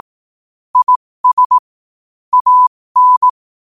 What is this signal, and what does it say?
Morse code, heard as a keyed tone, about 18 wpm, spaced as for 15 wpm, 1000 Hz. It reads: IS AN
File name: qotdfiles / 15wpm